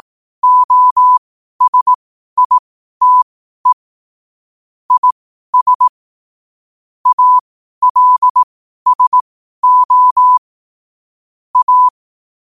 Morse code, heard as a keyed tone, about 18 wpm, spaced as for 12 wpm, 1000 Hz. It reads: OSITE IS ALSO A
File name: qotdfiles / 12wpm